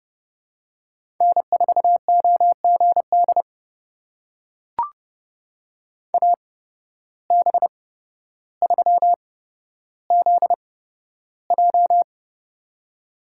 Morse code, heard as a keyed tone, about 30 wpm, spaced as for 13 wpm, 700 Hz. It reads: N4OGD AB3ZJ
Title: USA Call Signs - 8x Character Spacing - 30wpm